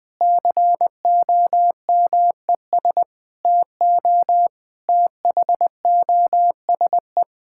Morse code, heard as a keyed tone, about 20 wpm, 700 Hz. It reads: COMES TO THOSE
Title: Difficult Sentences with Hints - 20wpm